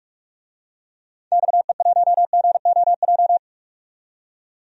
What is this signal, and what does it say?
Morse code, heard as a keyed tone, about 47 wpm, 700 Hz. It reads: XE1GOJ
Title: DXCC-100 Call Signs - 6 Characters - 45wpm